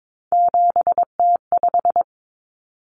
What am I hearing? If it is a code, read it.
Morse code, heard as a keyed tone, about 22 wpm, 700 Hz. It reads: 7T5
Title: Common Cut Numbers - 3-Digit 22wpm